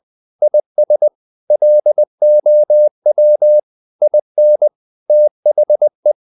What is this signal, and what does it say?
Morse code, heard as a keyed tone, about 20 wpm, 600 Hz. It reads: IS LOW IN THE
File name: the old man and the sea / the old man and the sea part 1 20wpm